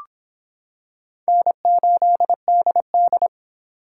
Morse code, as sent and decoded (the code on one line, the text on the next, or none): -. ---.. -.. -..
N8DD